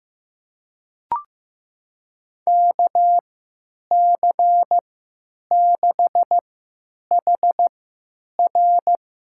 Morse code, heard as a keyed tone, about 15 wpm, spaced as for 9 wpm, 700 Hz. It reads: KC6HR
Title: USA Call Signs - 3x Character Spacing - 15wpm